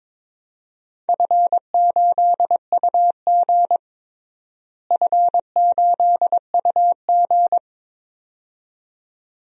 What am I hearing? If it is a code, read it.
Morse code, heard as a keyed tone, about 22 wpm, 700 Hz. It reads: F8UG F8UG
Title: DXCC-100 Call Signs - 4 Characters - Repeated 2x - 22wpm